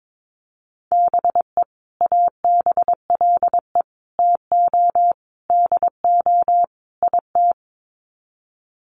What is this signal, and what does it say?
Morse code, heard as a keyed tone, about 22 wpm, 700 Hz. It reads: BE ABLE TO DO IT
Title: Sets of 5 Words Encore2 22wpm